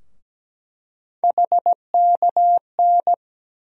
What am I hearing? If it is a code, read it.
Morse code, heard as a keyed tone, about 17 wpm, 700 Hz. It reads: HKN
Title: USA Call Sign Postfixes - 3 Letters 17wpm